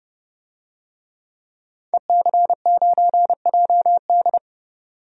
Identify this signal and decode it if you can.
Morse code, heard as a keyed tone, about 29 wpm, 690 Hz. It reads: EC9JD
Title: DXCC-100 Call Signs - 5 Characters - 30wpm